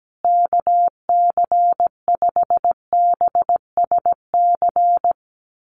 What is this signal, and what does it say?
Morse code, heard as a keyed tone, about 17 wpm, 700 Hz. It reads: KC5BSC
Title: USA Call Signs Repeated 3x 17wpm